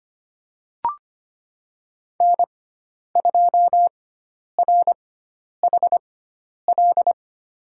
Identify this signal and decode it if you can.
Morse code, heard as a keyed tone, about 25 wpm, spaced as for 10 wpm, 700 Hz. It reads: N2RHL
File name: Lesson 16g - Letter c - Call Signs 25wpm